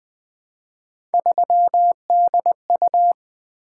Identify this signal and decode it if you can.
Morse code, heard as a keyed tone, about 20 wpm, 700 Hz. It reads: 3DU
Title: Three Letter-Number Combinations - 20wpm